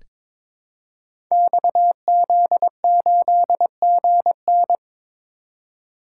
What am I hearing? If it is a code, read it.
Morse code, heard as a keyed tone, about 22 wpm, 700 Hz. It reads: XZ8GN
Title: Code Groups - 5 Letter:Numbers - 22wpm